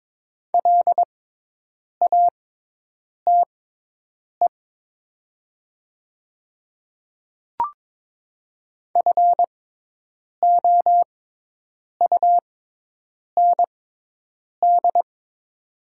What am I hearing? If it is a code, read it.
Morse code, heard as a keyed tone, about 22 wpm, spaced as for 8 wpm, 700 Hz. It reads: LATE FOUND
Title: 6x Character Spacing - Top 500 Words - 22wpm